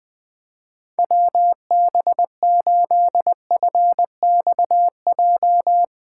Morse code, heard as a keyed tone, about 20 wpm, 700 Hz. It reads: WB8FXJ
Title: K1USN SST QSO Practice - 20wpm